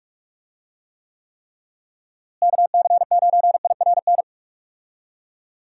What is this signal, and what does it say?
Morse code, heard as a keyed tone, about 46 wpm, 700 Hz. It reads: KC9IRN
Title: USA Call Signs - 6 Characters - Repeated 3x - 45wpm